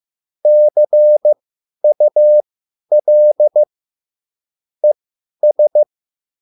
Morse code, heard as a keyed tone, about 15 wpm, spaced as for 10 wpm, 600 Hz. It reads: CUL ES
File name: QSL / CW lesson